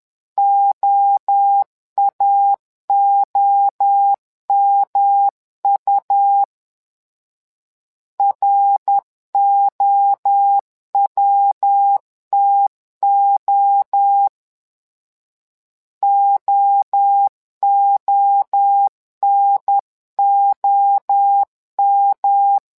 Morse code, heard as a keyed tone, about 10 wpm, 800 Hz. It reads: OAOMU ROWTO OONOM
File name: LEZIONI / 3 MWRO / 3 MWRO 10wpm 2148